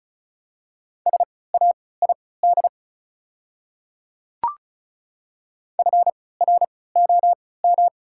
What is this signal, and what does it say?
Morse code, heard as a keyed tone, about 34 wpm, spaced as for 20 wpm, 700 Hz. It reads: SAID FROM
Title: 3x Character Spacing - Top 100 Words - 35wpm